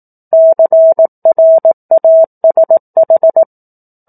Morse code, heard as a keyed tone, about 18 wpm, 650 Hz. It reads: CRASH
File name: 04-Chapter-04 18 WPM